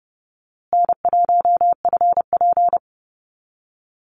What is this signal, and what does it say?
Morse code, heard as a keyed tone, about 30 wpm, 700 Hz. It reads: N1FP
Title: USA Call Signs Repeated 3x 30wpm